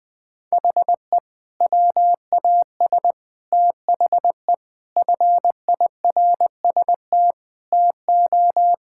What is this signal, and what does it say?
Morse code, heard as a keyed tone, about 21 wpm, 700 Hz. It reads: HE WAS THE FIRST TO
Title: Sentences with 2-Word Phrases - 20wpm